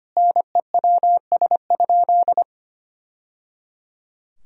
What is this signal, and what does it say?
Morse code, heard as a keyed tone, about 25 wpm, 700 Hz. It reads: NEWS?
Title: Sentences from Top 1000 Words - Review Part 4 - 25wpm